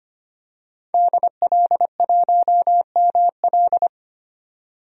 Morse code, heard as a keyed tone, about 25 wpm, 700 Hz. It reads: DL1ML
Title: DXCC-100 Call Signs - 5 Characters - 25wpm